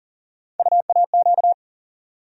Morse code, heard as a keyed tone, about 40 wpm, 700 Hz. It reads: UAQ
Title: USA Call Sign Postfixes - 3 Letters 40wpm